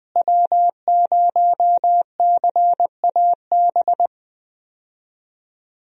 Morse code, heard as a keyed tone, about 20 wpm, 700 Hz. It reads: W0CAB
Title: USA Call Signs - 5 Characters - Repeated 3x - 20wpm